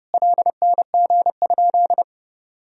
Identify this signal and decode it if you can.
Morse code, heard as a keyed tone, about 30 wpm, 700 Hz. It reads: LNG?
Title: Sentences from Top 200 Words and Abbreviations 30wpm